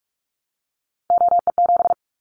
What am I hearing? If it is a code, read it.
Morse code, heard as a keyed tone, about 44 wpm, 700 Hz. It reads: OE7
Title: Callsign Prefixes for Top 25 DXCC - Repeated 3x - 45wpm